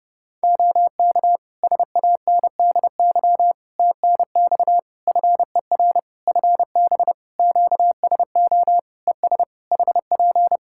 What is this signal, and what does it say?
Morse code, heard as a keyed tone, about 28 wpm, 700 Hz. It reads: OK SANDY TNX FER FB QSO ES HP
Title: K1USN SST QSO Practice 30wpm